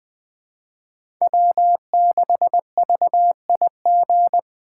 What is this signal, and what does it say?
Morse code, heard as a keyed tone, about 21 wpm, 700 Hz. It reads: W6VIG
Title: USA Call Signs - 5 Characters - 2x Character Spacing - 20wpm